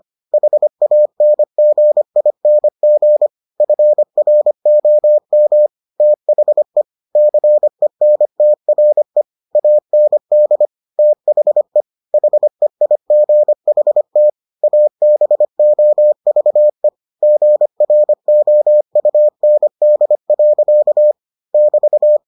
Morse code, heard as a keyed tone, about 25 wpm, 600 Hz. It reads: HANGING FROM THE CENTRE AND THE HEIGHT ABOVE GROUND. <BT>